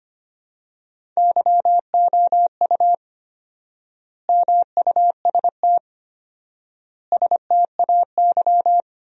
Morse code, heard as a keyed tone, about 25 wpm, 700 Hz. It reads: YOU MUST STAY
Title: 4x Word-Spacing - Sentences from Top 500 Words - 25wpm